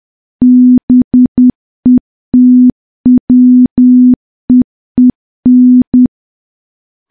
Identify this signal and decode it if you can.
Morse code, heard as a keyed tone, about 10 wpm, 250 Hz. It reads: BETWEEN
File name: morse / morse-training-10WPM